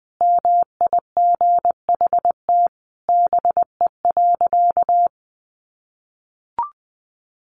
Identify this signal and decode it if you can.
Morse code, heard as a keyed tone, about 20 wpm, 700 Hz. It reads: MIGHT BE.
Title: Sentences from Top 1500 Words 20wpm